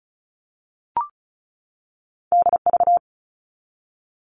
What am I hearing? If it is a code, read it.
Morse code, heard as a keyed tone, about 36 wpm, 700 Hz. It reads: DV
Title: All Cut Numbers - 2-Digit 35wpm